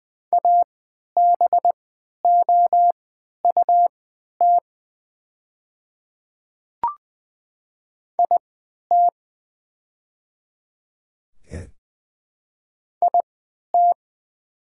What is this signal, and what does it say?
Morse code, heard as a keyed tone, about 20 wpm, spaced as for 11 wpm, 700 Hz. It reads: ABOUT IT IT
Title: 3x Character Spacing - Top 200 Words - 20wpm